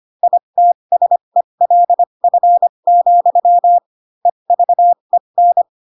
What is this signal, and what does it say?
Morse code, heard as a keyed tone, about 25 wpm, spaced as for 22 wpm, 700 Hz. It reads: ITSELF, EVEN